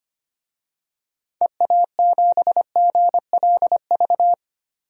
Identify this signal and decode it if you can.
Morse code, heard as a keyed tone, about 25 wpm, 700 Hz. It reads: EA7GLV
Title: DXCC-100 Call Signs - 6 Characters - 25wpm